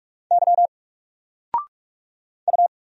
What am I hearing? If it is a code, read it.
Morse code, heard as a keyed tone, about 46 wpm, 700 Hz. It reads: Y U
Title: Single Letter-Number Alt 45wpm